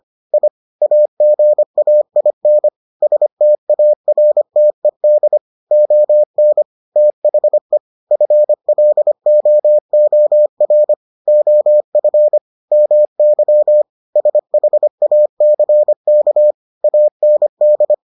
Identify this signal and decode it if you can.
Morse code, heard as a keyed tone, about 25 wpm, 600 Hz. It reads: I AGAIN STARTED ON THE FLOOR OF MY SHACK AND